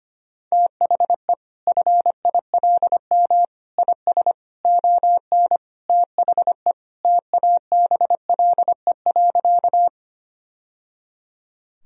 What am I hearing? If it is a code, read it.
Morse code, heard as a keyed tone, about 25 wpm, 700 Hz. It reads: THE FILM IS ON THE TABLE.
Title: Sentences from Top 1000 Words - Review Part 3 - 25wpm